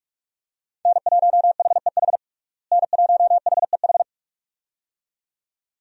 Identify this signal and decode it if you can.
Morse code, heard as a keyed tone, about 46 wpm, 700 Hz. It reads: N1HEH N1HEH
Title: USA Call Signs - 5 Characters - Repeated 2x - 45wpm